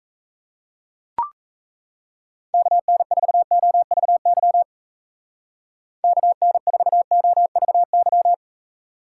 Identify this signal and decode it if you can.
Morse code, heard as a keyed tone, about 34 wpm, 700 Hz. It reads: KN4OVY KN4OVY
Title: Speed Racing - USA 2x3 Call Signs 28wpm